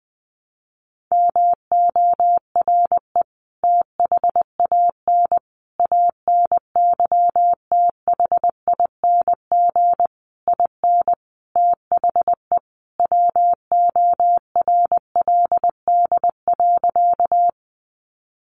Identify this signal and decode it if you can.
Morse code, heard as a keyed tone, about 20 wpm, 700 Hz. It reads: MORE THAN ANYTHING IN THE WORLD.
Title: Sentences from Top 1000 Words 20wpm